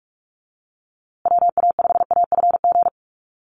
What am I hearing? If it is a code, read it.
Morse code, heard as a keyed tone, about 44 wpm, 700 Hz. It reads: WA5AFG
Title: USA Call Signs Repeated 2x 45wpm